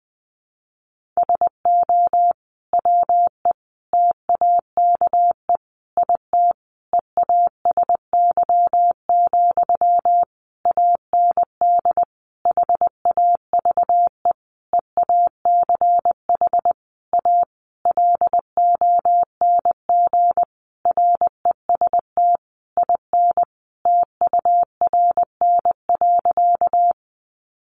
Morse code, heard as a keyed tone, about 20 wpm, 700 Hz. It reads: SO WE TAKE IT EASY, AND HAVE EACH A LONG REST IN TURN.
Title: Sentences from Top 700 Words 20wpm